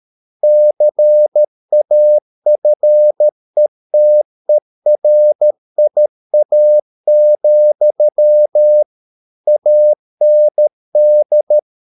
Morse code, heard as a keyed tone, about 13 wpm, 600 Hz. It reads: CAFETERIA, AND